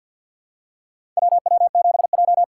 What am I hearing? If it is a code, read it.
Morse code, heard as a keyed tone, about 51 wpm, 700 Hz. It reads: WW7J
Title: USA Call Signs - 4x Character Spacing - 50wpm